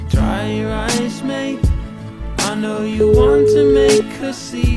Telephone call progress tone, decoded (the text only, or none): ringback tone